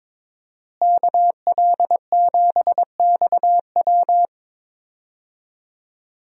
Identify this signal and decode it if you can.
Morse code, heard as a keyed tone, about 22 wpm, 700 Hz. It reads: KL7XW
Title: DXCC-100 Call Signs - 5 Characters - Repeated 2x - 22wpm